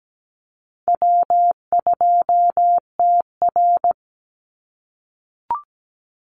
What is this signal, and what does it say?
Morse code, heard as a keyed tone, about 17 wpm, 700 Hz. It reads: W2TR
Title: Call Signs - 17wpm